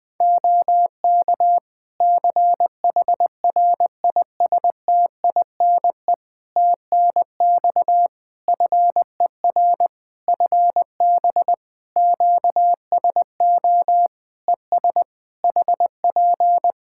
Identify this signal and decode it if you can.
Morse code, heard as a keyed tone, about 20 wpm, 700 Hz. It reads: OK CHRISTINE TNX FER FB QSO ES HP
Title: Practice QSO - Exchange 4 - Ending - 20wpm